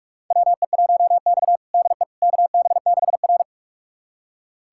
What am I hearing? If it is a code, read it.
Morse code, heard as a keyed tone, about 42 wpm, 700 Hz. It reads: WE1X DE KB6R
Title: USA Callsign DE Callsign - 4 Characters - 45wpm